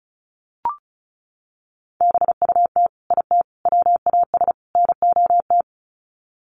Morse code, heard as a keyed tone, about 34 wpm, 700 Hz. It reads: BUT IT WAS NOT
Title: Sets of 4 Words 35wpm